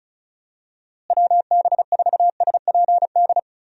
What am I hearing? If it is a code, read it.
Morse code, heard as a keyed tone, about 34 wpm, 700 Hz. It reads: WB4SPD
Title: USA Call Signs - 6 Characters - Repeated 2x - 35wpm